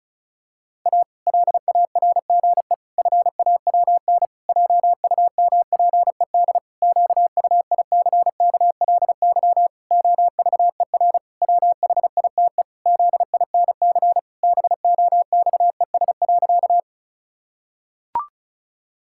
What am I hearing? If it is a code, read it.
Morse code, heard as a keyed tone, about 34 wpm, 700 Hz. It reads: A LARGE FAWN JUMPED QUICKLY OVER WHITE ZINC BOXES.
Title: Pangram Challenge v2 - 35wpm